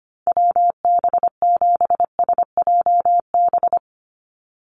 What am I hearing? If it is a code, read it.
Morse code, heard as a keyed tone, about 25 wpm, 700 Hz. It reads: WB7SJB
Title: USA Call Signs 25wpm